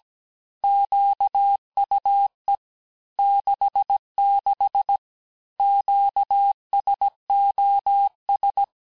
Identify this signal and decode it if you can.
Morse code, heard as a keyed tone, about 17 wpm, 800 Hz. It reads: QUE 66 QSOS
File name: corrige-cours-1581-fin-du-QSO-vitesse-16-mpm